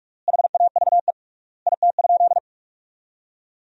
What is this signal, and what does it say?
Morse code, heard as a keyed tone, about 46 wpm, 700 Hz. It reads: HAVE IT?
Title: 3x Word-Spacing - Sentences from Top 100 Words - 45wpm